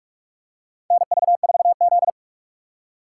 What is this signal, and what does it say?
Morse code, heard as a keyed tone, about 44 wpm, 690 Hz. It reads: NV4Z
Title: USA Call Signs - 6x Character Spacing - 45wpm